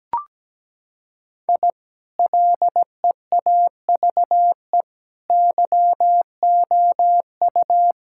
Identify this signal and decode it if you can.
Morse code, heard as a keyed tone, about 17 wpm, 700 Hz. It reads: I LEAVE YOU
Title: Sentences from Top 1000 Words - Review Part 5 - 17wpm